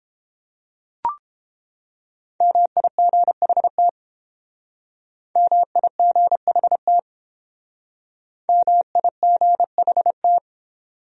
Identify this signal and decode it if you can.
Morse code, heard as a keyed tone, about 31 wpm, 700 Hz. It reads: MIGHT MIGHT MIGHT